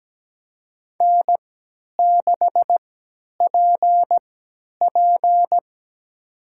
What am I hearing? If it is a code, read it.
Morse code, heard as a keyed tone, about 17 wpm, spaced as for 10 wpm, 700 Hz. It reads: N6PP
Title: USA Call Signs - 4 Characters - 3x Character Spacing - 17wpm